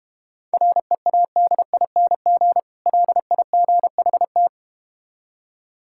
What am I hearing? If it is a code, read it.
Morse code, heard as a keyed tone, about 31 wpm, 700 Hz. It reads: READING LIGHT